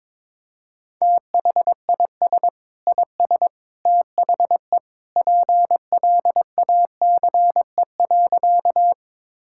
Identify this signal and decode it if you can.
Morse code, heard as a keyed tone, about 22 wpm, 700 Hz. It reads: THIS IS THE PLACE.
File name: practice / 22wpm / Sentences with 3-Word Phrases - 22wpm